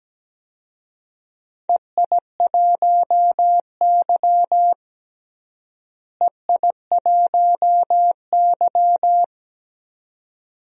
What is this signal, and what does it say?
Morse code, heard as a keyed tone, about 17 wpm, 700 Hz. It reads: EI1Y EI1Y
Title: DXCC-100 Call Signs - 4 Characters - Repeated 2x - 17wpm